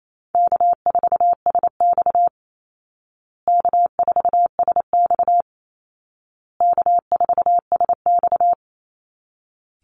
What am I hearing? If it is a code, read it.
Morse code, heard as a keyed tone, about 28 wpm, 700 Hz. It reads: K4SX K4SX K4SX
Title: Easy Call Signs Repeated 3x - 28wpm